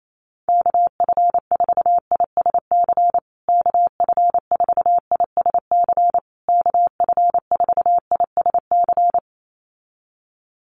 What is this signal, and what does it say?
Morse code, heard as a keyed tone, about 28 wpm, 700 Hz. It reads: KF4ISC KF4ISC KF4ISC
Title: USA Call Signs Repeated 3x 28wpm